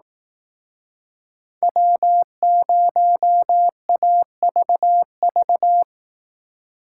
Morse code, heard as a keyed tone, about 18 wpm, 700 Hz. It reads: W0AVV
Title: Speed Racing - USA 1x3 Call Signs 15wpm